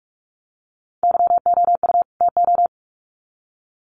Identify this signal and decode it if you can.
Morse code, heard as a keyed tone, about 44 wpm, 700 Hz. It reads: YOU TO
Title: Sets of 2 Words 45wpm